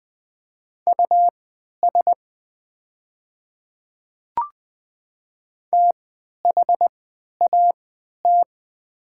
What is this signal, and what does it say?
Morse code, heard as a keyed tone, about 20 wpm, spaced as for 11 wpm, 690 Hz. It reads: US THAT
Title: 3x Character Spacing - Top 200 Words - 20wpm